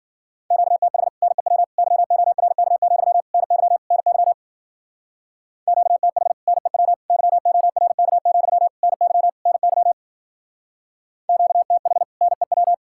Morse code, heard as a keyed tone, about 53 wpm, 700 Hz. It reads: QTH NEW YORK, NY NY QTH NEW YORK, NY NY QTH NEW